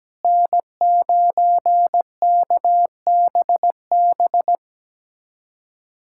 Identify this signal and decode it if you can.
Morse code, heard as a keyed tone, about 17 wpm, 700 Hz. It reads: N9KBB
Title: USA Callsign DE Callsign - 5 Characters - 17wpm